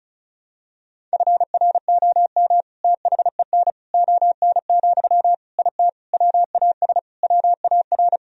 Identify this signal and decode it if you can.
Morse code, heard as a keyed tone, about 34 wpm, 700 Hz. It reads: FROM THEN ON, IT WAS WAR
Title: Sentences from Top 1000 Words - Review Part 2 - 35wpm